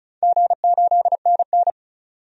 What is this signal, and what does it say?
Morse code, heard as a keyed tone, about 36 wpm, 700 Hz. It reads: G8NN
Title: DXCC-100 Call Signs - 4 Characters - 35wpm